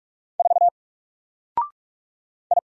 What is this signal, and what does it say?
Morse code, heard as a keyed tone, about 45 wpm, 700 Hz. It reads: 4 I
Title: Single Letter-Number 45wpm